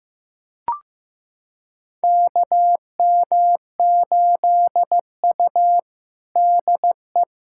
Morse code, heard as a keyed tone, about 15 wpm, 700 Hz. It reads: KM8U DE
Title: USA Callsign DE Callsign - 4 Characters - 15wpm